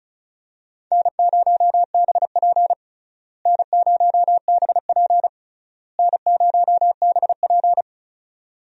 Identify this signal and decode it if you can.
Morse code, heard as a keyed tone, about 36 wpm, 700 Hz. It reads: N0BP N0BP N0BP
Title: USA Call Signs - 4 Characters - Repeated 3x - 35wpm